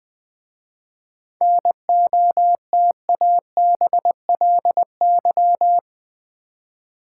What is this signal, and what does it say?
Morse code, heard as a keyed tone, about 20 wpm, 700 Hz. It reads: NOTABLY
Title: 7 Letter Words - First 3 Letters Given - 20wpm